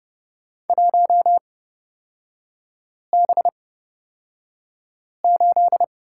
Morse code, heard as a keyed tone, about 30 wpm, spaced as for 5 wpm, 700 Hz. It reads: 1B8
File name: Lesson 37e - Letter x - Three Characters 30wpm